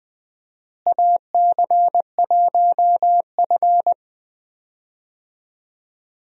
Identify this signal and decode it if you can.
Morse code, heard as a keyed tone, about 20 wpm, 700 Hz. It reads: AC1F
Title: USA Call Signs - 4 Characters - Repeated 2x - 20wpm